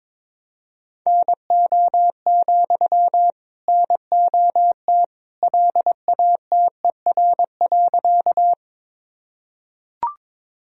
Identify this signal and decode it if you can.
Morse code, heard as a keyed tone, about 22 wpm, 700 Hz. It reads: NO, NOT LATER.
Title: Sentences from Top 1000 Words - Review Part 4 - 22wpm